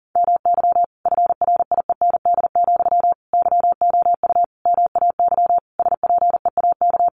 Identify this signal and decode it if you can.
Morse code, heard as a keyed tone, about 40 wpm, 700 Hz. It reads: MY FRIEND, YOU MAY SPEAK
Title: Sentences from Top 1300 Words 40wpm